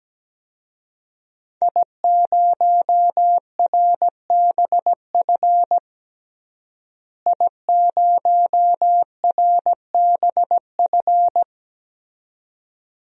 Morse code, heard as a keyed tone, about 17 wpm, 700 Hz. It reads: I0RBF I0RBF
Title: DXCC-100 Call Signs - 5 Characters - Repeated 2x - 17wpm